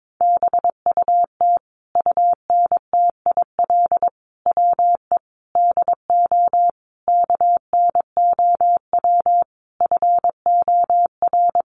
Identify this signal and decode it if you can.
Morse code, heard as a keyed tone, about 22 wpm, 700 Hz. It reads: BUT UNTIL WE DO KNOW FOR
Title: A Sound Of Thunder 22wpm